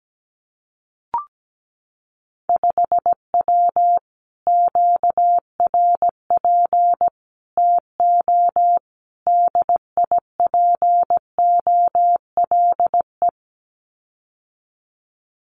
Morse code, heard as a keyed tone, about 17 wpm, 700 Hz. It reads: 5W QRP TO DIPOLE
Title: Common QSO Phrases Encore 17wpm